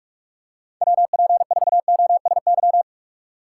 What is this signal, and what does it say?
Morse code, heard as a keyed tone, about 44 wpm, 700 Hz. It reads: WP4OSY